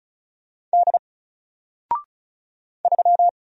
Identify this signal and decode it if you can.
Morse code, heard as a keyed tone, about 34 wpm, 700 Hz. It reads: D 3
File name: Single-Letter-Number-35wpm